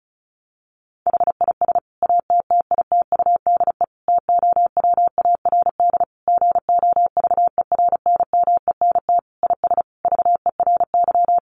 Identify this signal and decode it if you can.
Morse code, heard as a keyed tone, about 34 wpm, 700 Hz. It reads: HIS ATTITUDE TOWARD GOVERNMENT IS VERY
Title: Sentences from Top 1300 Words 35wpm